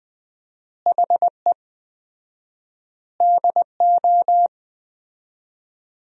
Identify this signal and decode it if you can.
Morse code, heard as a keyed tone, about 20 wpm, 700 Hz. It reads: HE DO
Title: 4x Word-Spacing - Sentences from Top 100 Words - 20wpm